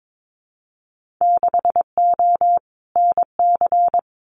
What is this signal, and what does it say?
Morse code, heard as a keyed tone, about 22 wpm, 700 Hz. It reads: 6O NC
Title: Winter Field Day Exchanges 22wpm